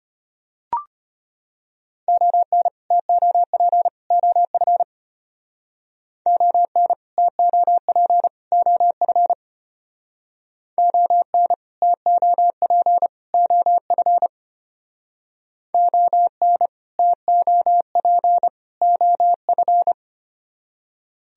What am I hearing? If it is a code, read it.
Morse code, heard as a keyed tone, about 32 wpm, 700 Hz. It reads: ON TOP OF ON TOP OF ON TOP OF ON TOP OF